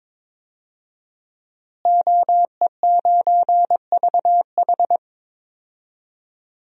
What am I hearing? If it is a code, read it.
Morse code, heard as a keyed tone, about 22 wpm, 700 Hz. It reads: OE9VH
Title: DXCC-100 Call Signs - 5 Characters - Repeated 3x - 22wpm